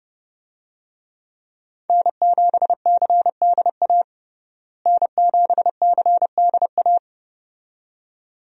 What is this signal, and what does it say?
Morse code, heard as a keyed tone, about 32 wpm, 700 Hz. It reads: N7CDA N7CDA
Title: USA Call Signs - 5 Characters - Repeated 2x - 30wpm